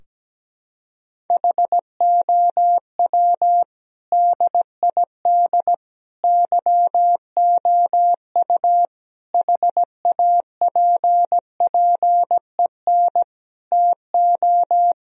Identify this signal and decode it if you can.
Morse code, heard as a keyed tone, about 17 wpm, 700 Hz. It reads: HOW DID YOU HAPPEN TO
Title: Sentences from Top 1000 Words - Review Part 4 - 17wpm